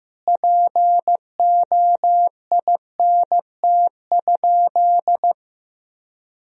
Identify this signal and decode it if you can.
Morse code, heard as a keyed tone, about 15 wpm, 700 Hz. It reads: POINT?
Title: Sentences from Top 1000 Words - Review Part 4 - 15wpm